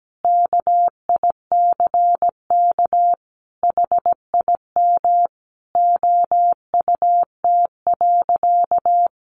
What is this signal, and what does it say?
Morse code, heard as a keyed tone, about 17 wpm, 700 Hz. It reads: KICK HIM OUT.
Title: Sentences from Top 1300 Words 17wpm